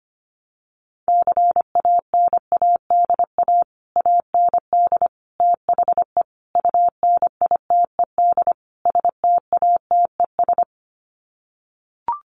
Morse code, heard as a keyed tone, about 25 wpm, 700 Hz. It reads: CANADA AND THE UNITED STATES
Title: Sets of 5 Words Encore 25wpm